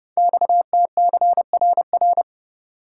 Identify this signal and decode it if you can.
Morse code, heard as a keyed tone, about 30 wpm, 700 Hz. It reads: XTCRR
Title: Code Groups - 5 Letters - 30wpm